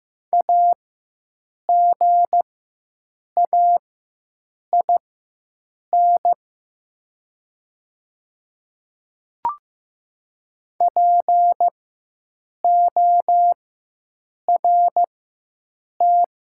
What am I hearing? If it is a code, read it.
Morse code, heard as a keyed tone, about 15 wpm, spaced as for 7 wpm, 700 Hz. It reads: AGAIN PORT
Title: 4x Character Spacing - Top 300 Words - 15wpm